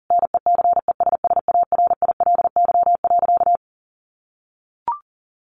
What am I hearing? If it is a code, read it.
Morse code, heard as a keyed tone, about 40 wpm, 700 Hz. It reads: NECESSARILY.
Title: Sentences from Top 1400 Words 40wpm